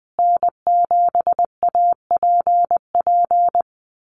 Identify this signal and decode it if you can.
Morse code, heard as a keyed tone, about 20 wpm, 700 Hz. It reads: N7APP
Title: USA Call Signs Repeated 2x 20wpm